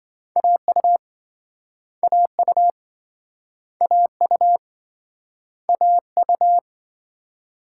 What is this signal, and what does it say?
Morse code, heard as a keyed tone, about 28 wpm, 700 Hz. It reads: AU AU AU AU